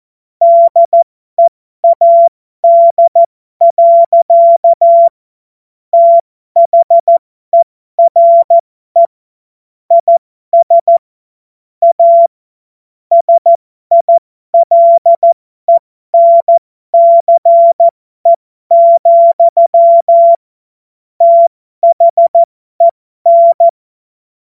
Morse code, heard as a keyed tone, about 14 wpm, spaced as for 12 wpm, 680 Hz. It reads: DEAD. THERE IS A SILENCE, THEN